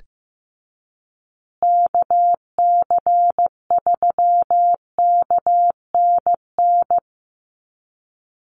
Morse code, heard as a keyed tone, about 15 wpm, 700 Hz. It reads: KC3KNN
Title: USA Call Signs Repeated 3x 15wpm